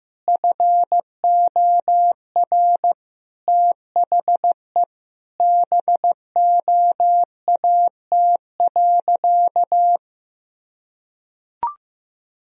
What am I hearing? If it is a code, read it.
Morse code, heard as a keyed tone, about 15 wpm, 700 Hz. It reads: FOR THE BOAT.
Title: Sentences from Top 1000 Words - Review Part 3 - 15wpm